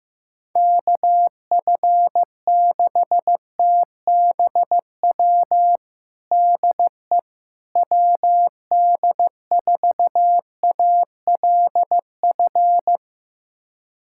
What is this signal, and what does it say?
Morse code, heard as a keyed tone, about 16 wpm, 700 Hz. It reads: KF6TBW DE WD4ALF
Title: USA Callsign DE Callsign - 6 Characters - 15wpm